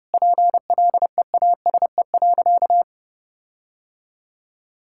Morse code, heard as a keyed tone, about 32 wpm, 700 Hz. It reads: PLEASE.
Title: Sentences from Top 1600 Words 30wpm